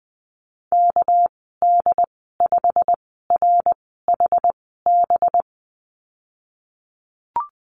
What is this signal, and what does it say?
Morse code, heard as a keyed tone, about 20 wpm, spaced as for 15 wpm, 700 Hz. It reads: KD5RHB
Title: 200 Lesson 32g - Letter k - Call Signs 20wpm